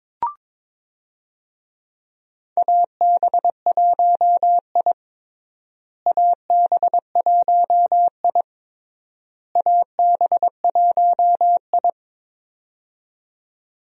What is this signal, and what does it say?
Morse code, heard as a keyed tone, about 22 wpm, 700 Hz. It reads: AB1I AB1I AB1I
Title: USA Call Signs - 4 Characters - Repeated 3x - 22wpm